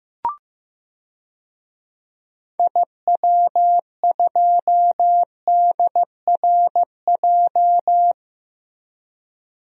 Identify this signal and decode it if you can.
Morse code, heard as a keyed tone, about 15 wpm, 700 Hz. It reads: IW2DRJ
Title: DXCC-100 Call Signs - 6 Characters - Repeated 3x - 15wpm